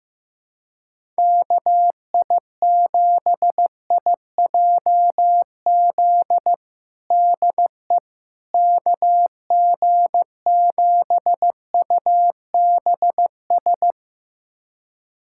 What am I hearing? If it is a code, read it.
Morse code, heard as a keyed tone, about 16 wpm, 700 Hz. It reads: KI7IJZ DE KG7UBS
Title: USA Callsign DE Callsign 15wpm